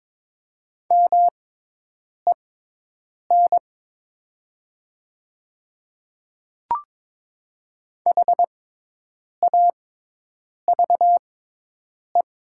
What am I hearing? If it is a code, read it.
Morse code, heard as a keyed tone, about 22 wpm, spaced as for 8 wpm, 700 Hz. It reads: MEN HAVE
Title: 6x Character Spacing - Top 200 Words - 22wpm